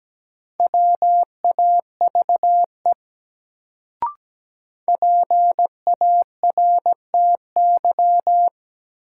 Morse code, heard as a keyed tone, about 17 wpm, 700 Hz. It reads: WAVE PARTY
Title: Sentences from Top 1400 Words - New Word Drill Alt - 17wpm